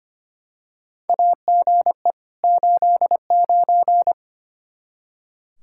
Morse code, heard as a keyed tone, about 25 wpm, 700 Hz. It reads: AGE 89
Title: K1USN SST QSO Practice 25wpm